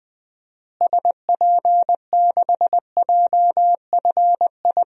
Morse code, heard as a keyed tone, about 20 wpm, 700 Hz. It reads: SP6JFI
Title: DXCC-100 Call Signs - 6 Characters - Repeated 2x - 20wpm